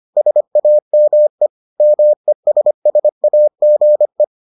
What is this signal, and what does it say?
Morse code, heard as a keyed tone, about 25 wpm, 600 Hz. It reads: SAME MESSAGE